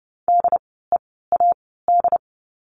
Morse code, heard as a keyed tone, about 30 wpm, 700 Hz. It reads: D E A D
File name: Lesson 16f - Letter c - Words 30wpm